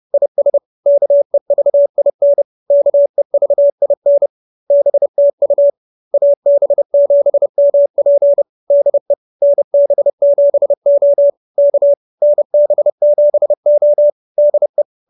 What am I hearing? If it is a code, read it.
Morse code, heard as a keyed tone, about 30 wpm, 570 Hz. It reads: IS KEVIN KEVIN BTU AB7MP DE NB7O K NB7O DE